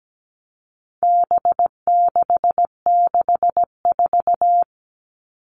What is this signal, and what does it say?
Morse code, heard as a keyed tone, about 17 wpm, 700 Hz. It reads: B664
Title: All Cut Numbers - 4-Digit 17wpm